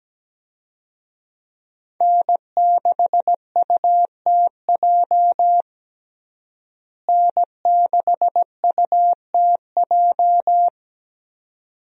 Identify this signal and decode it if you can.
Morse code, heard as a keyed tone, about 17 wpm, 700 Hz. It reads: N6UTJ N6UTJ
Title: USA Call Signs - 5 Characters - Repeated 2x - 17wpm